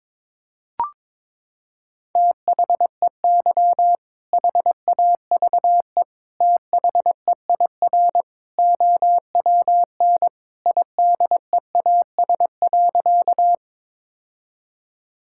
Morse code, heard as a keyed tone, about 22 wpm, 700 Hz. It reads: THEY HAVE THEIR OWN IDEAS.
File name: practice / 22wpm / Sentences with 2-Word Phrases - 22wpm